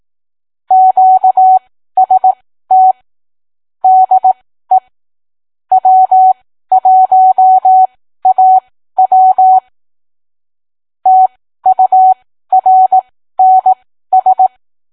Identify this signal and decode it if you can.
Morse code, heard as a keyed tone, about 18 wpm, spaced as for 13 wpm, 750 Hz. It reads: QST DE W1AW TURNS